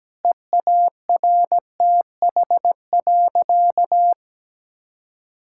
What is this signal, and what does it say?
Morse code, heard as a keyed tone, about 17 wpm, 700 Hz. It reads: EARTH.
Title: Sentences from Top 1000 Words - Review Part 5 - 17wpm